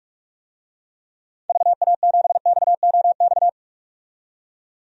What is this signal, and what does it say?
Morse code, heard as a keyed tone, about 46 wpm, 700 Hz. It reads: VA7XOX
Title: DXCC-100 Call Signs - 6 Characters - 45wpm